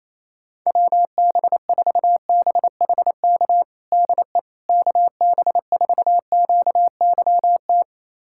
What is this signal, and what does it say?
Morse code, heard as a keyed tone, about 28 wpm, 700 Hz. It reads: WB4BHK DE KB4QYT
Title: USA Callsign DE Callsign - 6 Characters - 28wpm